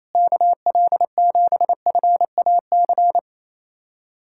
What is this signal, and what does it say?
Morse code, heard as a keyed tone, about 28 wpm, 700 Hz. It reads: KL7FAC
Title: USA Callsign DE Callsign 28wpm